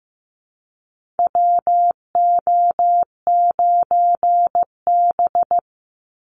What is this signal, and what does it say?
Morse code, heard as a keyed tone, about 15 wpm, 700 Hz. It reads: WO9B
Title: Call Signs Repeated 3x - 15wpm